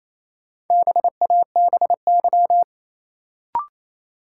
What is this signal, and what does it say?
Morse code, heard as a keyed tone, about 27 wpm, 700 Hz. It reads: BABY
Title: Sentences from Top 1300 Words - New Word Drill Alt - 28wpm